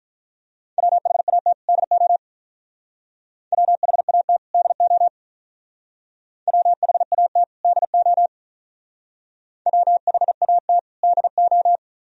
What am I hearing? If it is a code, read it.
Morse code, heard as a keyed tone, about 45 wpm, 700 Hz. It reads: WHAT DO WHAT DO WHAT DO WHAT DO